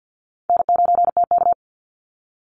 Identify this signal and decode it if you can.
Morse code, heard as a keyed tone, about 48 wpm, 700 Hz. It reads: N9TK
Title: Easy Call Signs - 50wpm